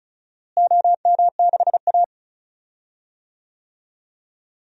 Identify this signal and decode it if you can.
Morse code, heard as a keyed tone, about 34 wpm, 700 Hz. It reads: OM6A
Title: DXCC-100 Call Signs - 4 Characters - Repeated 3x - 35wpm